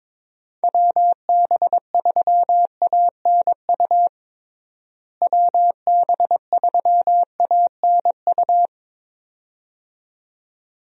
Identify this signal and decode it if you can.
Morse code, heard as a keyed tone, about 22 wpm, 700 Hz. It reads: WB3ANU WB3ANU
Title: USA Call Signs - 6 Characters - Repeated 2x - 22wpm